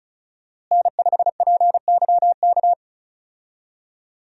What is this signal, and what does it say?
Morse code, heard as a keyed tone, about 34 wpm, 700 Hz. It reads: N5PYK
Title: USA Call Signs - 5 Characters - Repeated 2x - 35wpm